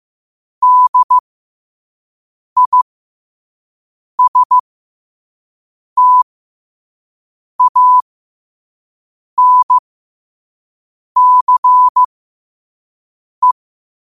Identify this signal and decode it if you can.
Morse code, heard as a keyed tone, about 15 wpm, spaced as for 5 wpm, 1000 Hz. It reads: DISTANCE
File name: qotdfiles / 5wpm